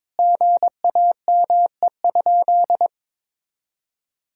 Morse code, heard as a keyed tone, about 22 wpm, 700 Hz. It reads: GAME?
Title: Sentences from Top 1000 Words - Review Part 4 - 22wpm